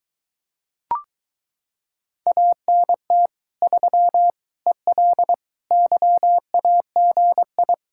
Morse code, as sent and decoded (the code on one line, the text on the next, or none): .- -. - / ...-- / . .-.. / -.-- .- --. ..
ANT 3 EL YAGI